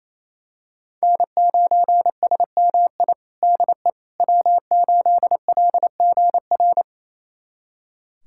Morse code, heard as a keyed tone, about 28 wpm, 700 Hz. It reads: N9SMI DE W8LGR
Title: USA Callsign DE Callsign - 5 Characters - 28wpm